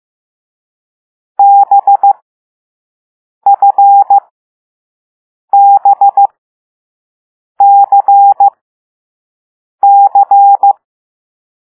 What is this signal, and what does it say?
Morse code, heard as a keyed tone, about 15 wpm, spaced as for 5 wpm, 800 Hz. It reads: BFBCC
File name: Morse / Level6B